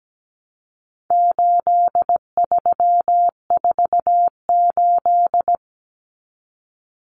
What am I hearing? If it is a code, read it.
Morse code, heard as a keyed tone, about 17 wpm, 700 Hz. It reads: 8348
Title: Common Cut Numbers - 4-Digit 17wpm